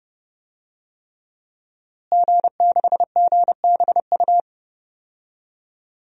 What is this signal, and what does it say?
Morse code, heard as a keyed tone, about 30 wpm, 700 Hz. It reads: G6GBU
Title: DXCC-100 Call Signs - 5 Characters - Repeated 3x - 30wpm